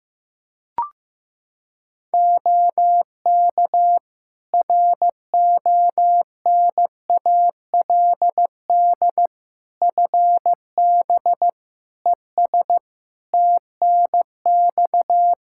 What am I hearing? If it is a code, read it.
Morse code, heard as a keyed tone, about 15 wpm, 700 Hz. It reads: OK RONALD FB ES TNX
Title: Practice QSO - Exchange 2 - Info - 15wpm